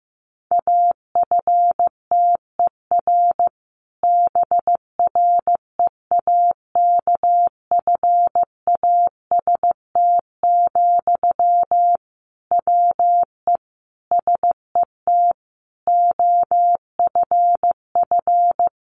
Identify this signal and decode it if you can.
Morse code, heard as a keyed tone, about 15 wpm, 700 Hz. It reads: AFTER BREAKFAST, WE SET OFF
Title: Sentences from Top 1500 Words 15wpm